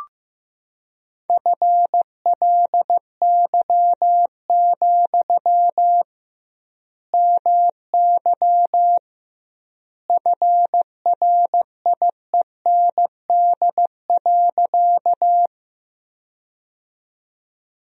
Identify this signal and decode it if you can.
Morse code, heard as a keyed tone, about 16 wpm, 700 Hz. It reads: FLY, MY FRIEND.
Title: 2x Word-Spacing - Sentences from Top 500 Words - 15wpm